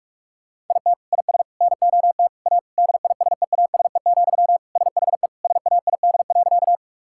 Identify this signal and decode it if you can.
Morse code, heard as a keyed tone, about 46 wpm, 700 Hz. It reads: IT IS NOT A DISEASE, SHE SAID.
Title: Sentences from Top 1000 Words - Review Part 6 - 45wpm